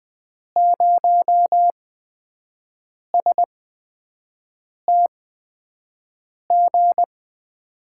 Morse code, heard as a keyed tone, about 20 wpm, spaced as for 5 wpm, 700 Hz. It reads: 0STG
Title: USA Call Signs - 8x Character Spacing - 20wpm